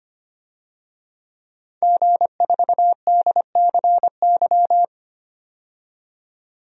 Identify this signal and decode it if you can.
Morse code, heard as a keyed tone, about 25 wpm, 700 Hz. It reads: G4DCY
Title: DXCC-100 Call Signs - 5 Characters - 25wpm